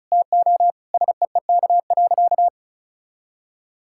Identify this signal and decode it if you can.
Morse code, heard as a keyed tone, about 36 wpm, 700 Hz. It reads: TO SEEK.
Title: Sentences from Top 1600 Words 35wpm